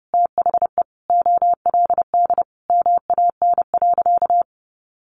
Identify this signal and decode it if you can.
Morse code, heard as a keyed tone, about 30 wpm, 700 Hz. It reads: THE OLD MAN.
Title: Sentences from Top 200 Words 30wpm